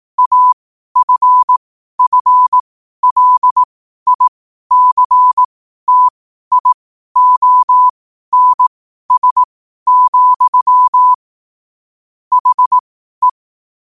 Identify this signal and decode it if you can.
Morse code, heard as a keyed tone, about 18 wpm, spaced as for 12 wpm, 1000 Hz. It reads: AFFLICTIONS, HE
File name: qotdfiles / 12wpm